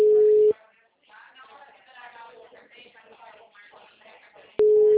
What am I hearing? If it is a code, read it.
Ringback tone (Europe, 425 Hz), with speech.